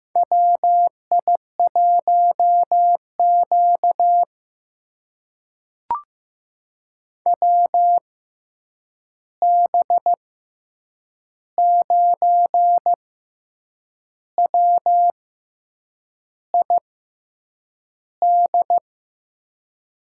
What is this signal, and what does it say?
Morse code, heard as a keyed tone, about 15 wpm, spaced as for 7 wpm, 700 Hz. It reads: WI1Q WB9WID
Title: USA Call Signs - 6x Character Spacing - 15wpm